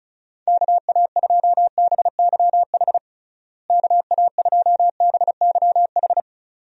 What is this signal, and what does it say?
Morse code, heard as a keyed tone, about 34 wpm, 700 Hz. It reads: KA2BYH KA2BYH
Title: USA Call Signs - 6 Characters - Repeated 2x - 35wpm